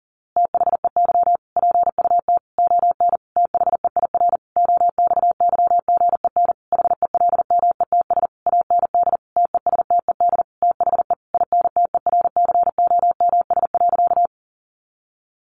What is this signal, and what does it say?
Morse code, heard as a keyed tone, about 37 wpm, 700 Hz. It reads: THEY PUT ON THEIR OXYGEN HELMETS AND TESTED THE INTERCOMS.
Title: A Sound Of Thunder 40wpm